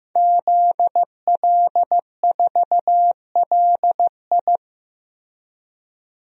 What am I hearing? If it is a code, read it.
Morse code, heard as a keyed tone, about 15 wpm, 700 Hz. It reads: ZL4LI
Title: DXCC-100 Call Signs - 5 Characters - 15wpm